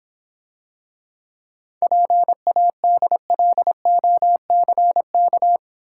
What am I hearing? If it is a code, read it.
Morse code, heard as a keyed tone, about 26 wpm, 700 Hz. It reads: PADLOCK